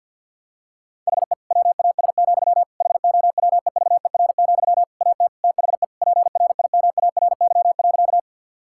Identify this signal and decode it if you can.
Morse code, heard as a keyed tone, about 50 wpm, 700 Hz. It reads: HE WAS, HOWEVER, AT THE PRIMARY.
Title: Sentences from Top 1000 Words - Review Part 5 - 50wpm